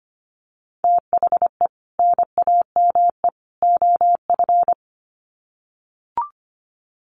Morse code, heard as a keyed tone, about 25 wpm, 700 Hz. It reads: THE NAME OF
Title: Sets of 3 Words 25wpm